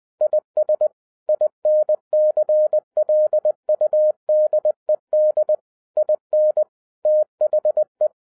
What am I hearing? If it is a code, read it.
Morse code, heard as a keyed tone, about 20 wpm, 610 Hz. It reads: IS INCLUDED IN THE